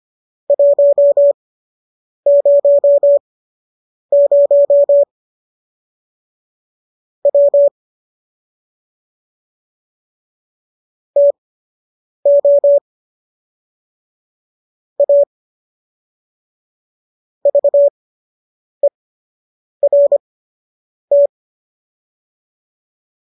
Morse code, heard as a keyed tone, about 25 wpm, spaced as for 8 wpm, 570 Hz. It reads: 100 W TO A VERT
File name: qso006 08